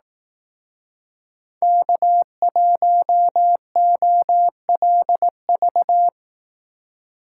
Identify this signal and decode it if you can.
Morse code, heard as a keyed tone, about 18 wpm, 700 Hz. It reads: K1OLV
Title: Speed Racing - USA 1x3 Call Signs 15wpm